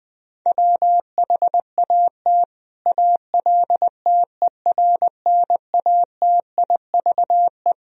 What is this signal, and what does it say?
Morse code, heard as a keyed tone, about 20 wpm, 700 Hz. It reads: WHAT ALTERNATIVE
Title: Sentences from Top 1000 Words - Review Part 4 - 20wpm